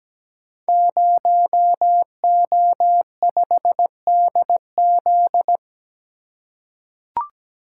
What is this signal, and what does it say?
Morse code, heard as a keyed tone, about 17 wpm, 700 Hz. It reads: 0O5DZ
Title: Code Groups - 5 Letter:Numbers - 17wpm